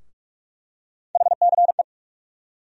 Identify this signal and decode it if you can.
Morse code, heard as a keyed tone, about 46 wpm, 700 Hz. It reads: HCE
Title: USA Call Sign Postfixes - 3 Letters 45wpm